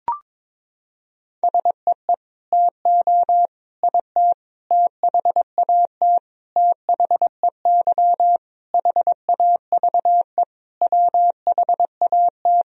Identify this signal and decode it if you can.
Morse code, heard as a keyed tone, about 22 wpm, 700 Hz. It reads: SEE TO IT THAT THEY HAVE WHAT
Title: Sentences from Top 1000 Words - Review Part 2 - 22wpm